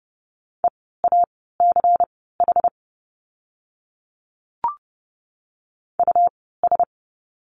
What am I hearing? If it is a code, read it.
Morse code, heard as a keyed tone, about 30 wpm, spaced as for 17 wpm, 700 Hz. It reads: EACH US
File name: course / 30:10wpm / 230 Lesson 37f - Letter x - Words 30wpm